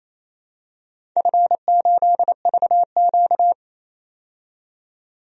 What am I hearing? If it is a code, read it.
Morse code, heard as a keyed tone, about 28 wpm, 700 Hz. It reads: F8VQ
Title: DXCC-100 Call Signs - 4 Characters - 28wpm